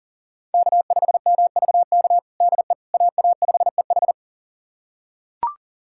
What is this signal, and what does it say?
Morse code, heard as a keyed tone, about 40 wpm, 700 Hz. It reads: K5MVK DE AA5EH
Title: USA Callsign DE Callsign - 5 Characters - 40wpm